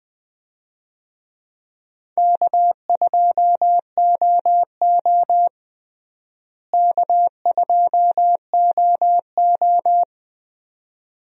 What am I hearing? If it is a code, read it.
Morse code, heard as a keyed tone, about 20 wpm, 700 Hz. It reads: K2OO K2OO
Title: USA Call Signs - 4 Characters - Repeated 2x - 20wpm